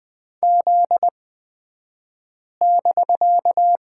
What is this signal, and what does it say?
Morse code, heard as a keyed tone, about 20 wpm, spaced as for 5 wpm, 700 Hz. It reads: Z<BK>
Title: 249 Lesson 40d - BK - Two Characters 20wpm